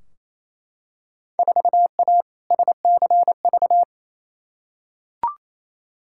Morse code, heard as a keyed tone, about 28 wpm, 700 Hz. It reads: 4A SCV
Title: Field Day Exchanges Encore - 28wpm